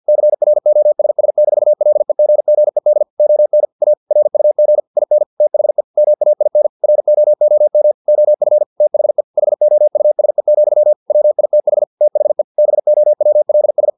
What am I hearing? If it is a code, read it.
Morse code, heard as a keyed tone, about 50 wpm, 600 Hz. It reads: CROSS-LEGGED ON A RUG IN THE MAIN ROOM OF THE HOUSE, WITH THE BOWLS
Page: Never 49wpm